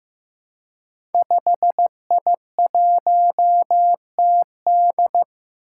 Morse code, heard as a keyed tone, about 15 wpm, 700 Hz. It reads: 5I1TD
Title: Five Challenging Letter-Number Combinations 15wpm